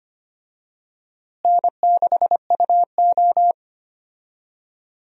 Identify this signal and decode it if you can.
Morse code, heard as a keyed tone, about 25 wpm, 700 Hz. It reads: N6UO
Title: USA Call Signs - 4 Characters - Repeated 3x - 25wpm